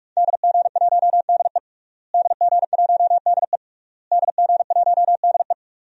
Morse code, heard as a keyed tone, about 43 wpm, 700 Hz. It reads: DG1DE DG1DE DG1DE
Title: DXCC-100 Call Signs - 5 Characters - Repeated 3x - 45wpm